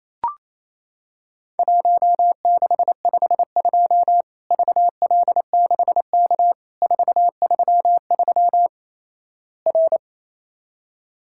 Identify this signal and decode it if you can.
Morse code, heard as a keyed tone, about 28 wpm, 700 Hz. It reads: 1652 VL6K 433 R
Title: Worked All Europe DX Contest - QTC Exchange - 28wpm